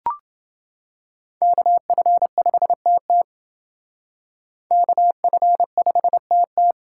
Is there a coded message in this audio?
Yes: Morse code from 1.4 to 6.7 s, heard as a keyed tone, about 28 wpm, 700 Hz. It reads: KF5TT KF5TT